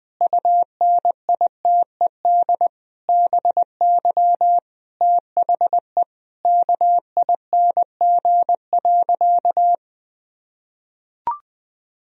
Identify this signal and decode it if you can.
Morse code, heard as a keyed tone, about 21 wpm, 700 Hz. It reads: UNITED BY THE KING.
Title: Sentences from Top 1000 Words - Review Part 4 - 20wpm